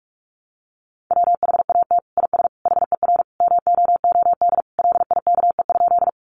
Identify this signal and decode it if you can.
Morse code, heard as a keyed tone, about 44 wpm, 700 Hz. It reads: WHAT IS HER MOOD LIKE?
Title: Sentences from Top 1100 Words 45wpm